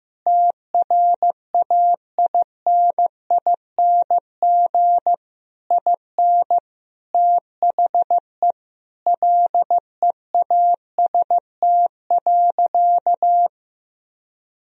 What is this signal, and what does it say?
Morse code, heard as a keyed tone, about 15 wpm, 700 Hz. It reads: TRAINING IN THE LEAST.
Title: Sentences from Top 1000 Words - Review Part 4 - 15wpm